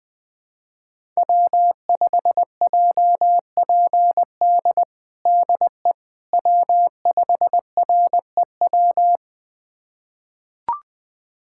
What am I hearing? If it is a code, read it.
Morse code, heard as a keyed tone, about 20 wpm, 700 Hz. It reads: W5JPD DE W5REW
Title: USA Callsign DE Callsign - 5 Characters - 20wpm